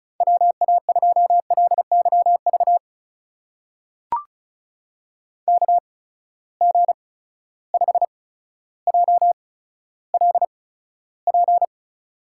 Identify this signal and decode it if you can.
Morse code, heard as a keyed tone, about 36 wpm, spaced as for 15 wpm, 700 Hz. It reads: WA2LYV KG5JLP
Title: USA Call Signs - 6 Characters - 8x Character Spacing - 35wpm